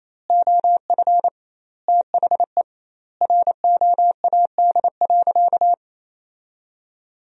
Morse code, heard as a keyed tone, about 28 wpm, 700 Hz. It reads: OF THE ROAD.
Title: 2x Word-Spacing - Sentences from Top 400 Words - 28wpm